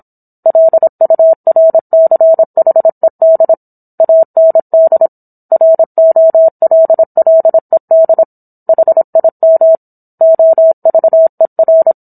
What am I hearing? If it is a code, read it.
Morse code, heard as a keyed tone, about 27 wpm, 650 Hz. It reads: LURCHED AND ROLLED HIM OVER
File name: Book One / 17-Chapter 17 - 26 WPM